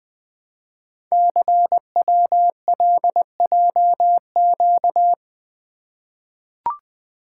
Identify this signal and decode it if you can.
Morse code, heard as a keyed tone, about 20 wpm, 700 Hz. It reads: CWLJQ
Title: Code Groups - 5 Letters - 20wpm